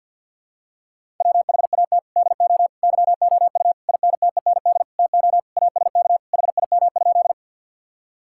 Morse code, heard as a keyed tone, about 48 wpm, 700 Hz. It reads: WHAT DO YOU INTEND TO ASK HIM?
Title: Sentences from Top 1600 Words 50wpm